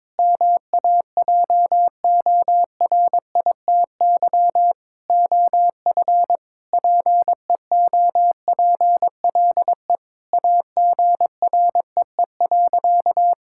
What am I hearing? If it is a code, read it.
Morse code, heard as a keyed tone, about 22 wpm, 700 Hz. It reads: MAJORITY OF PEOPLE AGREE.
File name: practice / 22wpm / Sentences with 4-Word Phrases - 22wpm